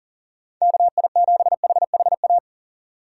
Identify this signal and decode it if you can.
Morse code, heard as a keyed tone, about 37 wpm, 700 Hz. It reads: KI7HHA
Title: USA Call Signs - 6 Characters - 3x Character Spacing - 40wpm